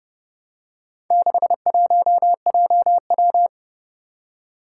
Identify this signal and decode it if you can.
Morse code, heard as a keyed tone, about 30 wpm, 700 Hz. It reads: 61JW
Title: Four Challenging Letter-Number Combinations 30wpm